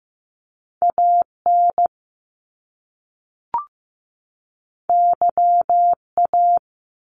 Morse code, heard as a keyed tone, about 15 wpm, 700 Hz. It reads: AN YA
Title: Top Two Letter Combinations in Words 15wpm